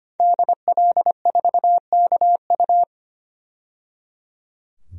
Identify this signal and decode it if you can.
Morse code, heard as a keyed tone, about 25 wpm, 700 Hz. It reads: DL4KU
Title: DXCC-100 Call Signs - 5 Characters - 25wpm